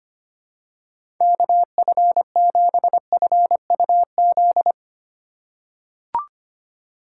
Morse code, heard as a keyed tone, about 25 wpm, 700 Hz. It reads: KF7FUZ
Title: USA Call Signs - 6 Characters - 2x Character Spacing - 25wpm